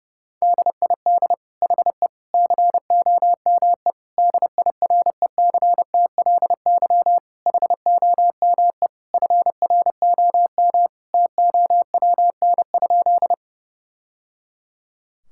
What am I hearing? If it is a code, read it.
Morse code, heard as a keyed tone, about 29 wpm, 700 Hz. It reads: DID HE COME DIRECTLY HOME FROM TOWN?
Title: Sentences from Top 1000 Words - Review Part 5 - 30wpm